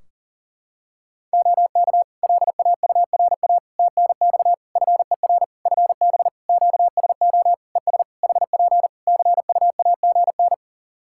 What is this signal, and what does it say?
Morse code, heard as a keyed tone, about 40 wpm, 700 Hz. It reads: OK LAURA TNX FER FB QSO ES HP CUAGN
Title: K1USN SST QSO Practice 40wpm